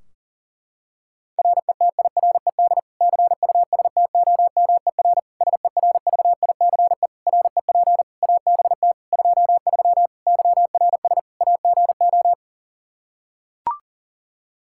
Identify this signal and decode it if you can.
Morse code, heard as a keyed tone, about 37 wpm, 700 Hz. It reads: RETIRED CUSTOMER SERVICE REP ABT 23 YRS AGO
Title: K1USN SST QSO Practice 40wpm